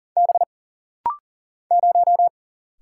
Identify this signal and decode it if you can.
Morse code, heard as a keyed tone, about 40 wpm, 700 Hz. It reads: B 0
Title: Single Letter-Number Alt - Rapid-Fire 40wpm